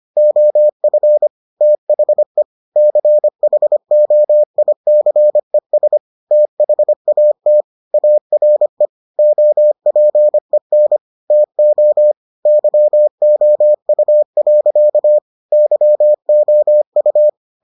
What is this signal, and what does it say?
Morse code, heard as a keyed tone, about 25 wpm, 600 Hz. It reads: OF THE CHOICES THAT ARE OPEN TO YOU. YOU